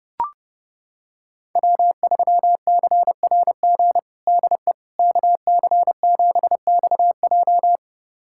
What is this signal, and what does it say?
Morse code, heard as a keyed tone, about 32 wpm, 700 Hz. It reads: W3CRG DE KC7XJ
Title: USA Callsign DE Callsign - 5 Characters - 30wpm